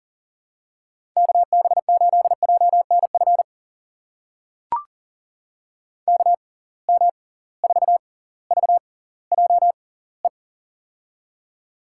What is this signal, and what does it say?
Morse code, heard as a keyed tone, about 40 wpm, spaced as for 21 wpm, 700 Hz. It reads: KB8JNF KM4VJE
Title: USA Call Signs - 6 Characters - 6x Character Spacing - 40wpm